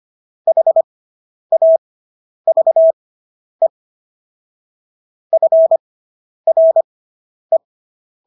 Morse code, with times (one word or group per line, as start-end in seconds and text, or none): 0.5-3.7 HAVE
5.3-7.6 FRE